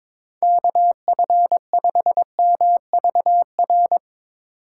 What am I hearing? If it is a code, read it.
Morse code, heard as a keyed tone, about 22 wpm, 700 Hz. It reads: KF5MVR
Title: USA Call Signs - 6 Characters - 4x Character Spacing - 22wpm